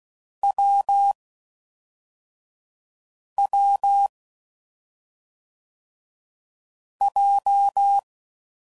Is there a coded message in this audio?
Yes: Morse code from 0.4 to 8.0 s, heard as a keyed tone, about 16 wpm, 800 Hz. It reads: W W J